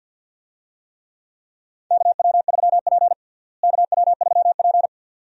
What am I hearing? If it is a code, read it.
Morse code, heard as a keyed tone, about 51 wpm, 700 Hz. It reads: KW3P KW3P
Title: USA Call Signs - 4 Characters - Repeated 2x - 50wpm